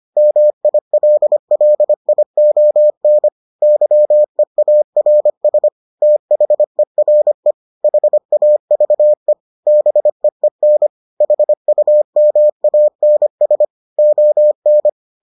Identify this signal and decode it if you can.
Morse code, heard as a keyed tone, about 25 wpm, 600 Hz. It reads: MILLION YEARS THERE HAVE BEEN HUMANS ON